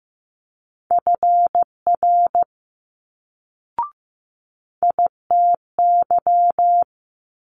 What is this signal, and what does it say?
Morse code, heard as a keyed tone, about 15 wpm, 700 Hz. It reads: FR ITY
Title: Multi-letter Phonemes Quick Fire 15wpm